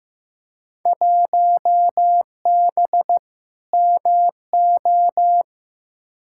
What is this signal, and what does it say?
Morse code, heard as a keyed tone, about 15 wpm, 700 Hz. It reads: 1B MO
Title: Field Day Exchanges - 15wpm